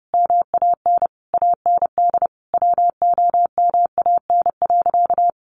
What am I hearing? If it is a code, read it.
Morse code, heard as a keyed tone, about 30 wpm, 700 Hz. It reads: MAN AND WOMAN.
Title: Sentences from Top 1100 Words 30wpm